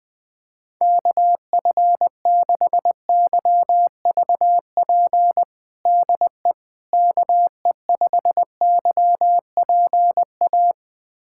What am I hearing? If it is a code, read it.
Morse code, heard as a keyed tone, about 21 wpm, 700 Hz. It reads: KF6YVP DE KE5YPA
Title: USA Callsign DE Callsign 20wpm